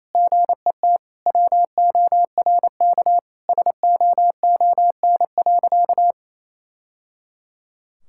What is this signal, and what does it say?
Morse code, heard as a keyed tone, about 28 wpm, 700 Hz. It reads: GET WORK SOON.
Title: Sentences from Top 1000 Words - Review Part 6 - 28wpm